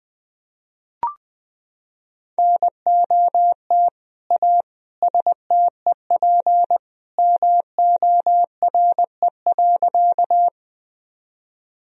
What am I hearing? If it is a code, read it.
Morse code, heard as a keyed tone, about 20 wpm, 700 Hz. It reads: NOT A STEP MORE.
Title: Sentences from Top 1000 Words - Review Part 2 - 20wpm